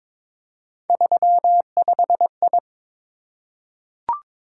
Morse code, heard as a keyed tone, about 22 wpm, 700 Hz. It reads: 35I
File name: practice / 22wpm / Three Letter-Number Combinations - 22wpm